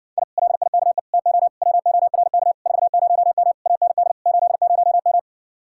Morse code, heard as a keyed tone, about 69 wpm, 700 Hz. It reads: I LIKE TO WORK 40M AND 80M